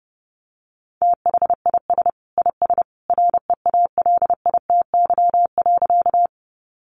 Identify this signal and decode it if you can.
Morse code, heard as a keyed tone, about 28 wpm, 700 Hz. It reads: THIS IS REALITY.
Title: Sentences from Top 1200 Words 30wpm